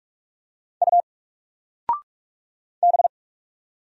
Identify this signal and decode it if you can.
Morse code, heard as a keyed tone, about 44 wpm, 700 Hz. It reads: U B
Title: Single Letter-Number Alt 45wpm